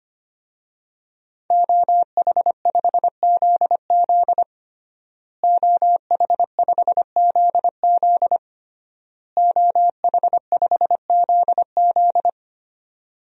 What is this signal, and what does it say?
Morse code, heard as a keyed tone, about 25 wpm, 700 Hz. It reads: OH5ZZ OH5ZZ OH5ZZ
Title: DXCC-100 Call Signs - 5 Characters - Repeated 3x - 25wpm